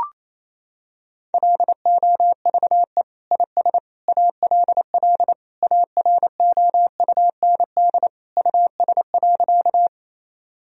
Morse code, heard as a keyed tone, about 28 wpm, 700 Hz. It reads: LOVE IS ALL AROUND US.
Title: Sentences from Top 1000 Words - Review Part 3 - 28wpm